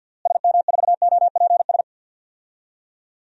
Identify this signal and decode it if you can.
Morse code, heard as a keyed tone, about 50 wpm, 700 Hz. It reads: SM4OPS
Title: DXCC-100 Call Signs - 6 Characters - Repeated 3x - 50wpm